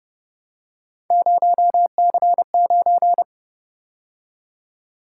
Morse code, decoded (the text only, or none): 0C9